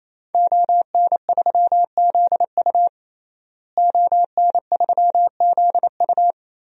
Morse code, heard as a keyed tone, about 28 wpm, 700 Hz. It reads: ON3ZU ON3ZU
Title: DXCC-100 Call Signs - 5 Characters - Repeated 3x - 28wpm